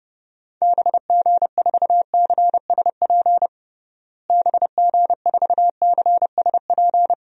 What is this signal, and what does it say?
Morse code, heard as a keyed tone, about 30 wpm, 700 Hz. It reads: BG4CSP BG4CSP
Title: DXCC-100 Call Signs - 6 Characters - Repeated 3x - 30wpm